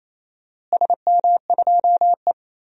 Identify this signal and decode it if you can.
Morse code, heard as a keyed tone, about 28 wpm, 700 Hz. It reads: SM2E
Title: DXCC-100 Call Signs - 4 Characters - Repeated 2x - 28wpm